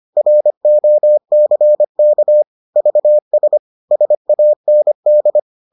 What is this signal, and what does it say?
Morse code, heard as a keyed tone, about 25 wpm, 600 Hz. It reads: ROCK VS SAND